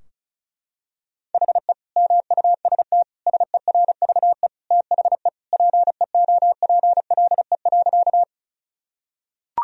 Morse code, heard as a keyed tone, about 36 wpm, 700 Hz. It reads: HE MUST SERVE THE PEOPLE.
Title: Sentences from Top 1000 Words - Review Part 2 - 35wpm